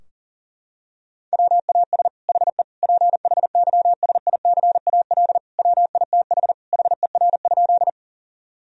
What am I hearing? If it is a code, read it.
Morse code, heard as a keyed tone, about 40 wpm, 700 Hz. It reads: WAS HE PHYSICAL WITH HER?
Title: Sentences from Top 1000 Words - Review Part 4 - 40wpm